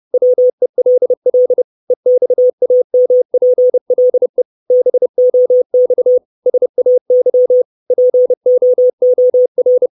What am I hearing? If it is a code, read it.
Morse code, heard as a keyed tone, about 30 wpm, 500 Hz. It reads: WELL EXAMPLE BOX SAY POOR
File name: 632 Words 30WPM